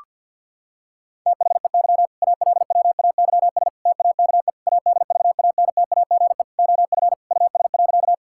Morse code, heard as a keyed tone, about 50 wpm, 700 Hz. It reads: THEY ALWAYS TAKE ADVANTAGE OF US.
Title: Sentences from Top 1000 Words - Review Part 6 - 50wpm